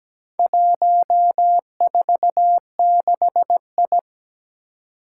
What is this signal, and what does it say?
Morse code, heard as a keyed tone, about 17 wpm, 700 Hz. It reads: 146I
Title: Four Challenging Letter-Number Combinations 17wpm